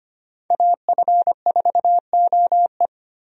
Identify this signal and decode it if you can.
Morse code, heard as a keyed tone, about 25 wpm, 700 Hz. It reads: AF4OE
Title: USA Call Signs - 5 Characters - 25wpm